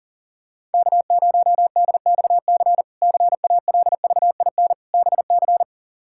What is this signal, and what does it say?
Morse code, heard as a keyed tone, about 40 wpm, 700 Hz. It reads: K0DXC CALVIN BC
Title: K1USN SST QSO Practice - 40wpm